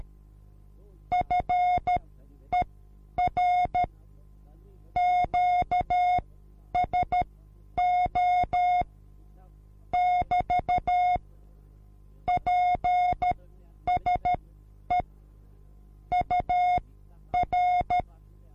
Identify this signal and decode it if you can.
Morse code, heard as a keyed tone, about 13 wpm, spaced as for 9 wpm, 720 Hz. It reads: FER QSO <BT> PSE UR
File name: repos / Morsekurs / DJ4UF Morsen24